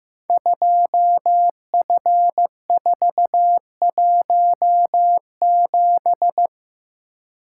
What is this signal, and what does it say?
Morse code, heard as a keyed tone, about 15 wpm, 700 Hz. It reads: 2F417
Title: Code Groups - 5 Letter:Numbers - 15wpm